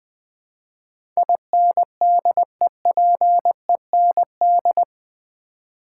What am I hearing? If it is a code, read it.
Morse code, heard as a keyed tone, about 20 wpm, 700 Hz. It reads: INDEPEND
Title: Build A Word - 4 Letters At A Time - 12-Letter Words - 20wpm